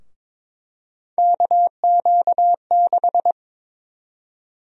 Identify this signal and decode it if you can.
Morse code, heard as a keyed tone, about 21 wpm, 700 Hz. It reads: KQ6
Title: USA Call Sign Prefixes - Repeated 2x - 22wpm